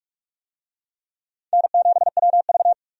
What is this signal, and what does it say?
Morse code, heard as a keyed tone, about 46 wpm, 700 Hz. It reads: N7WV
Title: USA Call Signs - 4 Characters - Repeated 2x - 45wpm